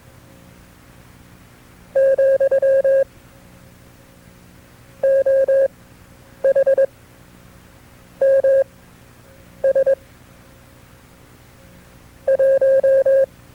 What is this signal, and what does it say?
Morse code, heard as a keyed tone, about 22 wpm, spaced as for 7 wpm, 540 Hz. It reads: , OHMS 1
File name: FreeFiles / CW-practice / KB6MT-tape2-side2